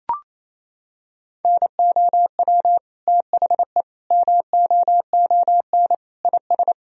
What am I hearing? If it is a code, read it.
Morse code, heard as a keyed tone, about 28 wpm, 700 Hz. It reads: NOW THE MOON IS
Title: Sentences from Top 1000 Words - Review Part 3 - 28wpm